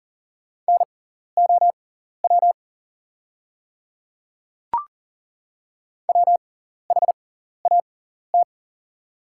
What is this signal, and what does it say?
Morse code, heard as a keyed tone, about 40 wpm, spaced as for 14 wpm, 700 Hz. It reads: NOW WHAT
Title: 6x Character Spacing - Top 300 Words - 40wpm